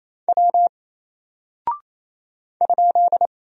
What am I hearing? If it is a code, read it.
Morse code, heard as a keyed tone, about 28 wpm, 700 Hz. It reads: W ?
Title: Single Character (Letter:Number:Punctuation:Prosign) 28wpm